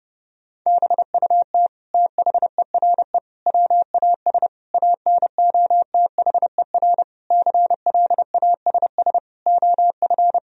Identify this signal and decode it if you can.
Morse code, heard as a keyed tone, about 29 wpm, 700 Hz. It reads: BUT THERE WAS ANOTHER CLASS OF
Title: Sentences from Top 1000 Words - Review Part 2 - 30wpm